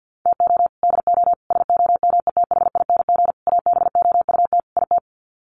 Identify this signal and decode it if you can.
Morse code, heard as a keyed tone, about 50 wpm, 700 Hz. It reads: TO DO SOMETHING ABOUT IT
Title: Sets of 5 Words 50wpm